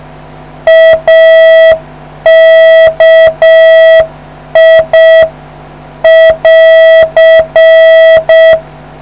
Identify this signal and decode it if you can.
Morse code, heard as a keyed tone, about 6 wpm, 650 Hz. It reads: AKI <AR>